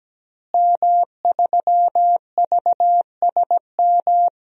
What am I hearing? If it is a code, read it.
Morse code, heard as a keyed tone, about 17 wpm, 700 Hz. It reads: M3VSM
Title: DXCC-100 Call Signs - 5 Characters - Repeated 3x - 17wpm